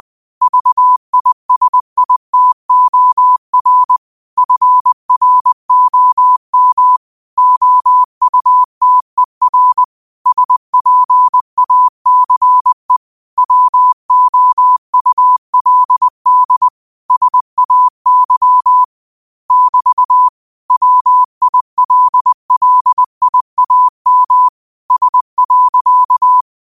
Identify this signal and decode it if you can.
Morse code, heard as a keyed tone, about 20 wpm, 1000 Hz. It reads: VISITOR FROM OUTER SPACE WOULD SAY <BT> WILLIAM S.
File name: qotdfiles / 20wpm